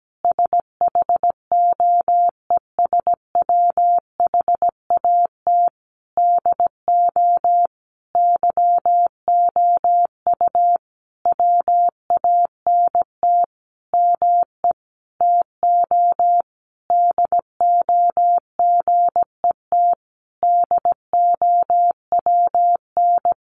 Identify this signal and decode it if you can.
Morse code, heard as a keyed tone, about 17 wpm, 700 Hz. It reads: SHOES WHAT DO YOU WANT ME TO DO GET DOWN
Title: A Sound Of Thunder 17wpm